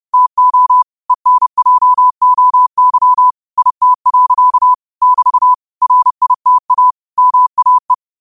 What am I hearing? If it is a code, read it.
Morse code, heard as a keyed tone, about 30 wpm, 1000 Hz. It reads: TO ENJOY IT. <BT> RITA MAE
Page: New quotes every day in morse code at 30 Words per minute.